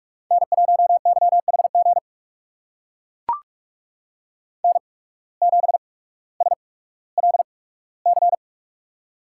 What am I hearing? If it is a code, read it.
Morse code, heard as a keyed tone, about 44 wpm, spaced as for 19 wpm, 700 Hz. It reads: N1YHG N7SLC
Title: USA Call Signs - 5 Characters - 8x Character Spacing - 45wpm